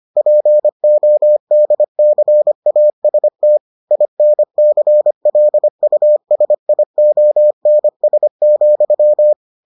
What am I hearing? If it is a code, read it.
Morse code, heard as a keyed tone, about 25 wpm, 600 Hz. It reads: PODCAST INCLUSIONS,